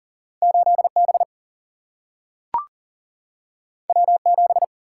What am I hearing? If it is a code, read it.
Morse code, heard as a keyed tone, about 40 wpm, 700 Hz. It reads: 8 B W 7
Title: Single-Letter Single-Number 40wpm